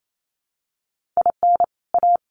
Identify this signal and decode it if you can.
Morse code, heard as a keyed tone, about 28 wpm, 700 Hz. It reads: IN A
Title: Sentences from Top 1500 Words 28wpm